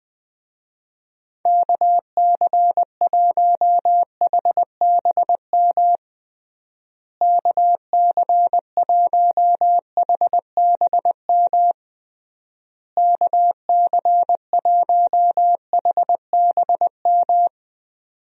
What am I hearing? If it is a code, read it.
Morse code, heard as a keyed tone, about 20 wpm, 700 Hz. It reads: KC1HBM KC1HBM KC1HBM
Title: USA Call Signs - 6 Characters - Repeated 3x - 20wpm